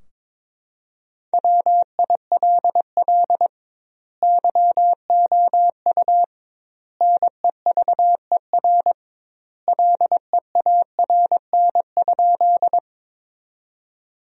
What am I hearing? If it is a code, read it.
Morse code, heard as a keyed tone, about 21 wpm, 700 Hz. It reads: WILL YOU NEVER LEARN?
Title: 2x Word-Spacing - Sentences from Top 300 Words - 22wpm